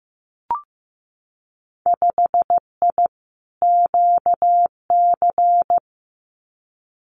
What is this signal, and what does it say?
Morse code, heard as a keyed tone, about 15 wpm, 700 Hz. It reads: 5I QC
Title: Winter Field Day Exchanges Encore 15wpm